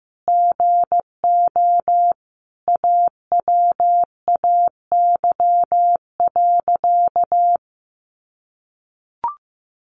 Morse code, heard as a keyed tone, about 14 wpm, 700 Hz. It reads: GO AWAY.
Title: Sentences from Top 1500 Words 15wpm